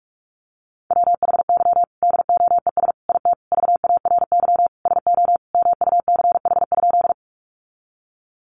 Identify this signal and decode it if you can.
Morse code, heard as a keyed tone, about 44 wpm, 700 Hz. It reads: WHY DOES IT VARY SO MUCH?
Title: Sentences from Top 1200 Words 45wpm